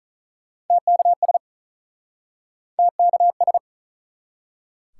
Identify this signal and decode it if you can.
Morse code, heard as a keyed tone, about 48 wpm, 700 Hz. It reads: TKS TKS